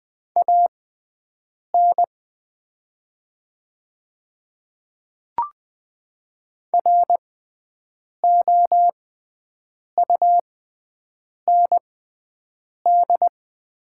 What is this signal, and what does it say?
Morse code, heard as a keyed tone, about 20 wpm, spaced as for 7 wpm, 700 Hz. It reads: AN ROUND
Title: 6x Character Spacing - Top 300 Words - 20wpm